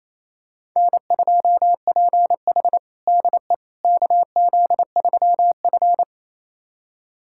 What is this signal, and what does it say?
Morse code, heard as a keyed tone, about 28 wpm, 700 Hz. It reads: N2PH DE KZ3F
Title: USA Callsign DE Callsign - 4 Characters - 28wpm